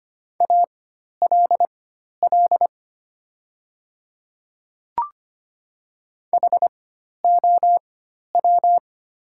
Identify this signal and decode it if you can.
Morse code, heard as a keyed tone, about 25 wpm, spaced as for 12 wpm, 700 Hz. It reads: ALL HOW
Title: 4x Character Spacing - Top 100 Words - 25wpm